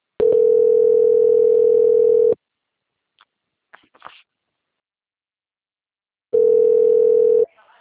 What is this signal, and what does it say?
Ringback tone (North America), 440 + 480 Hz.
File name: rg-1600-unknown- - - .66723